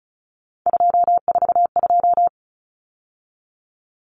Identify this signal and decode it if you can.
Morse code, heard as a keyed tone, about 34 wpm, 700 Hz. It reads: 242
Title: Numbers - 3-Digit 35wpm